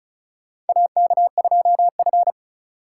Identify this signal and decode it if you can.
Morse code, heard as a keyed tone, about 36 wpm, 700 Hz. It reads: AK2F
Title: USA Call Signs - 4x Character Spacing - 35wpm